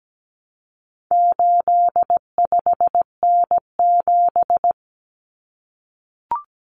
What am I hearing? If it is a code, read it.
Morse code, heard as a keyed tone, about 17 wpm, 700 Hz. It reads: 85N7
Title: Common Cut Numbers - 4-Digit 17wpm